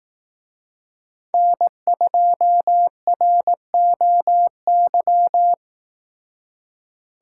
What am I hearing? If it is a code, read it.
Morse code, heard as a keyed tone, about 18 wpm, 700 Hz. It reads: N2ROY
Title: Speed Racing - USA 1x3 Call Signs 15wpm